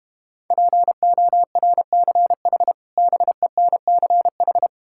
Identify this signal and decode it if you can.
Morse code, heard as a keyed tone, about 32 wpm, 700 Hz. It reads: PORCH BENCH